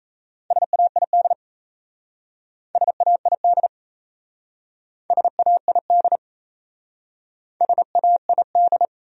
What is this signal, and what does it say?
Morse code, heard as a keyed tone, about 36 wpm, 700 Hz. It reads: SAID SAID SAID SAID